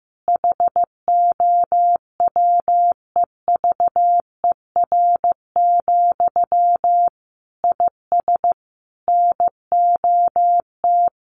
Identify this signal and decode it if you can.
Morse code, heard as a keyed tone, about 15 wpm, 700 Hz. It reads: HOWEVER, IS NOT
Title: Sentences from Top 1200 Words 15wpm